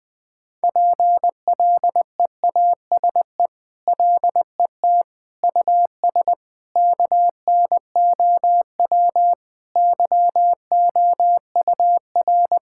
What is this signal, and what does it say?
Morse code, heard as a keyed tone, about 20 wpm, 700 Hz. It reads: PLEASE LET US KNOW YOUR
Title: Sentences with 8-Word Phrases - Spoken Beginning - 20wpm